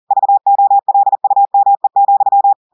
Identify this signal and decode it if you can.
Morse code, heard as a keyed tone, about 40 wpm, 800 Hz. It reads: VOLUME,
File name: audios / portugais / portugais texte en clair 1 vitesse 40 mots minutes